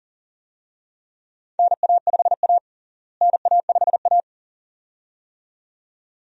Morse code, heard as a keyed tone, about 37 wpm, 700 Hz. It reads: NA5A NA5A
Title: USA Call Signs - 4 Characters - Repeated 2x - 40wpm